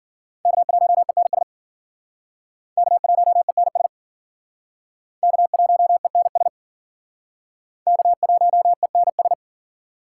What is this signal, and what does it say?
Morse code, heard as a keyed tone, about 53 wpm, 700 Hz. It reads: K1ENS K1ENS K1ENS K1ENS